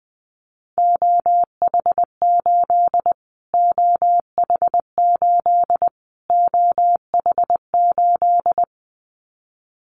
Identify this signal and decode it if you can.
Morse code, heard as a keyed tone, about 20 wpm, 700 Hz. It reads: OH8 OH8 OH8
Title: Callsign Prefixes for Top 25 DXCC - Repeated 3x - 20wpm